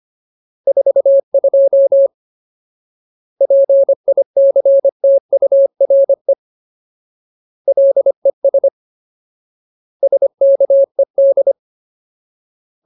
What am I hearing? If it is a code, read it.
Morse code, heard as a keyed tone, about 25 wpm, 550 Hz. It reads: 42 PICTURE LES SKED
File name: Ultimate Word List / 25WPM Ultimate